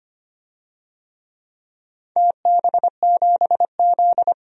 Morse code, heard as a keyed tone, about 25 wpm, 700 Hz. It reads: TB7Z
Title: DXCC-100 Call Signs - 4 Characters - 25wpm